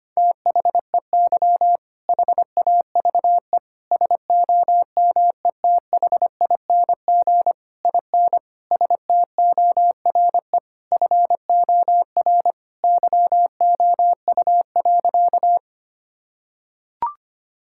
Morse code, heard as a keyed tone, about 25 wpm, 700 Hz. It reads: THEY HAVE SOMETHING IN STORE FOR YOU.
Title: Sentences from Top 1000 Words - Review Part 5 - 25wpm